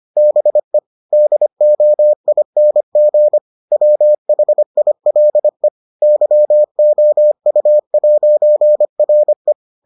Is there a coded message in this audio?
Yes: Morse code from 0.2 to 9.5 s, heard as a keyed tone, about 25 wpm, 600 Hz. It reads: BE DOING WHILE YOU'RE